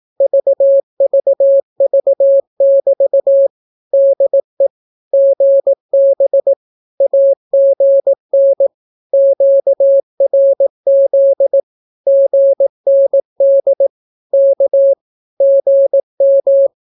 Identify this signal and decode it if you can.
Morse code, heard as a keyed tone, about 18 wpm, 550 Hz. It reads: VVV<BT> DE GB AGN QRZ GND K GM
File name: naukapodstaw / skroty / skroty